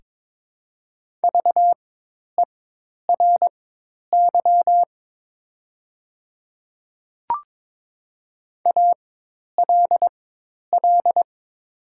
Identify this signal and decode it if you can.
Morse code, heard as a keyed tone, about 22 wpm, spaced as for 10 wpm, 700 Hz. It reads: VERY ALL
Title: 4x Character Spacing - Top 200 Words - 22wpm